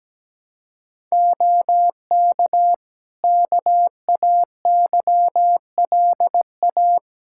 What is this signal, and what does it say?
Morse code, heard as a keyed tone, about 17 wpm, 700 Hz. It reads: OK KAYLA
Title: K1USN SST QSO Practice 17wpm